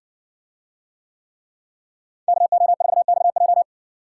Morse code, heard as a keyed tone, about 60 wpm, 700 Hz. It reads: KO4XJ